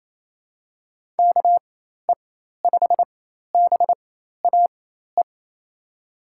Morse code, heard as a keyed tone, about 27 wpm, spaced as for 13 wpm, 700 Hz. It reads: KE5BAE
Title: USA Call Signs - 4x Character Spacing - 28wpm